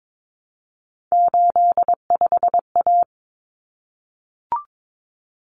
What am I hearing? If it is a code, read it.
Morse code, heard as a keyed tone, about 22 wpm, 700 Hz. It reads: 85A
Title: Common Cut Numbers - 3-Digit 22wpm